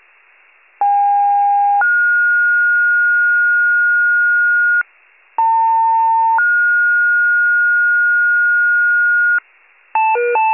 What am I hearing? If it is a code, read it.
Two-tone page, heard as a one-second tone, then a three-second tone, about 810 then 1430 Hz, about 900 then 1430 Hz. It is heard from 0.8 to 9.4 s.
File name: pager / 78 FIRE / wcalert